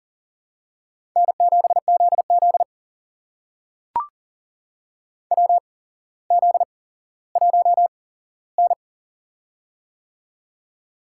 Morse code, heard as a keyed tone, about 40 wpm, spaced as for 17 wpm, 700 Hz. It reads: N7ZZ WZ1N
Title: USA Call Signs - 4 Characters - 8x Character Spacing - 40wpm